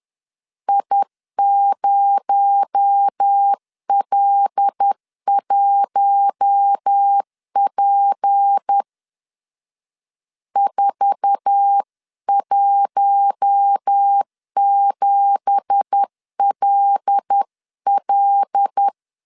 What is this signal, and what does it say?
Morse code, heard as a keyed tone, about 10 wpm, 800 Hz. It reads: I0L1P 417LL
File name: CW / DET / 1 9 1500C S A 10wpm 4511